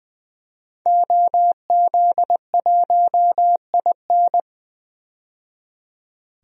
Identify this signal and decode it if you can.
Morse code, heard as a keyed tone, about 20 wpm, 700 Hz. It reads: OZ1IN
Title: DXCC-100 Call Signs - 5 Characters - 20wpm